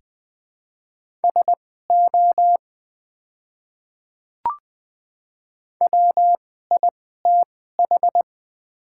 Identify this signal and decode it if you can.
Morse code, heard as a keyed tone, about 20 wpm, spaced as for 15 wpm, 700 Hz. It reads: SO WITH
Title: 2x Character Spacing - Top 300 Words - 20wpm